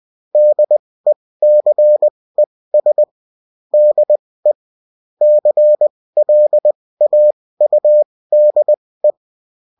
Morse code, heard as a keyed tone, about 20 wpm, spaced as for 16 wpm, 600 Hz. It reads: DECES DE CLAUDE